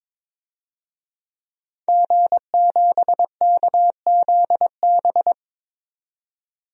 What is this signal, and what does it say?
Morse code, heard as a keyed tone, about 22 wpm, 700 Hz. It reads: G7KZB
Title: DXCC-100 Call Signs - 5 Characters - 22wpm